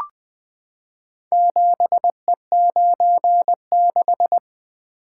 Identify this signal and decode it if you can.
Morse code, heard as a keyed tone, about 20 wpm, 700 Hz. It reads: 7E96
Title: Four Letter-Number Combinations - 20wpm